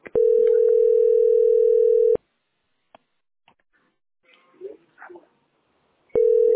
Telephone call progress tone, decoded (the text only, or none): ringback tone